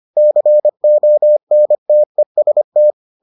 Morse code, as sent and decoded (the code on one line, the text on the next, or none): -.-. --- -. - . ... -
CONTEST